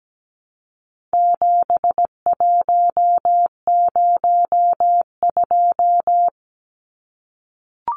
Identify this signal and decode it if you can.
Morse code, heard as a keyed tone, about 17 wpm, 700 Hz. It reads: 7102
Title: Numbers - 4-Digit 17wpm